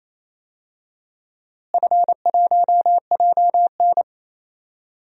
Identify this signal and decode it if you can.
Morse code, heard as a keyed tone, about 28 wpm, 700 Hz. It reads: F1JN
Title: DXCC-100 Call Signs - 4 Characters - Repeated 3x - 28wpm